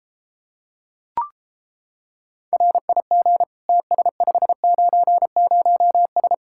Morse code, heard as a keyed tone, about 29 wpm, 700 Hz. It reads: RIG TS590S